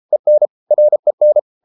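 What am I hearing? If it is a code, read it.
Morse code, heard as a keyed tone, about 33 wpm, 600 Hz. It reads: EN REN
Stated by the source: Morse code at 33 wpm, QRQ.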